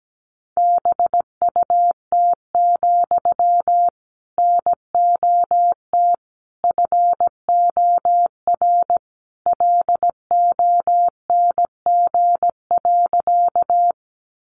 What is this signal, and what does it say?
Morse code, heard as a keyed tone, about 17 wpm, 700 Hz. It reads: BUT, NOT FOR LONG.
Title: Sentences from Top 600 Words 17wpm